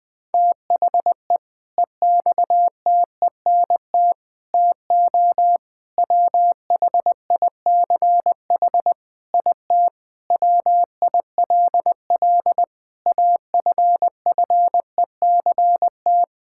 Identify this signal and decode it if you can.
Morse code, heard as a keyed tone, about 20 wpm, 700 Hz. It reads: THE EXTENT TO WHICH IT WILL AFFECT
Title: Sentences with 4-Word Phrases - 20wpm